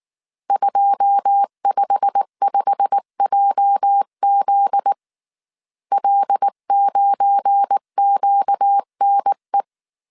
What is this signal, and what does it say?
Morse code, heard as a keyed tone, about 19 wpm, 800 Hz. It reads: 255JZ L9QNE